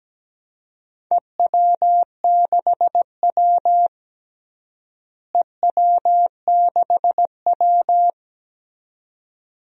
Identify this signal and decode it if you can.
Morse code, heard as a keyed tone, about 17 wpm, 700 Hz. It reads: EW6W EW6W
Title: DXCC-100 Call Signs - 4 Characters - Repeated 2x - 17wpm